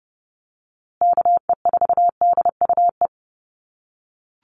Morse code, heard as a keyed tone, about 30 wpm, 700 Hz. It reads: KE4DUE
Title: USA Call Signs Repeated 2x 30wpm